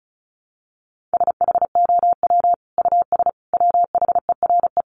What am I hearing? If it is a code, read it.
Morse code, heard as a keyed tone, about 34 wpm, 700 Hz. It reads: SHOW US WHERE
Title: Sentences from Top 200 Words 35wpm